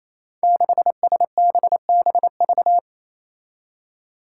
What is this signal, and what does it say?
Morse code, heard as a keyed tone, about 27 wpm, 700 Hz. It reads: 6SBBV
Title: Five Challenging Letter-Number Combinations 28wpm